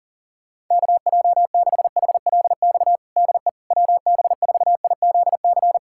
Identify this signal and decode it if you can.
Morse code, heard as a keyed tone, about 37 wpm, 700 Hz. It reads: KJ6HLX DE WB4IZC
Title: USA Callsign DE Callsign - 6 Characters - 40wpm